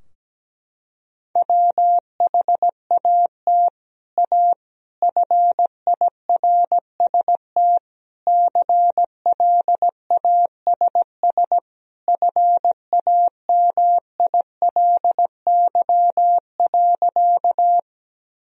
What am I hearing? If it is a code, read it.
Morse code, heard as a keyed tone, about 17 wpm, 700 Hz. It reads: WHAT A FIRST CLASS FAMILY.
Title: Sentences from Top 1000 Words - Review Part 2 - 17wpm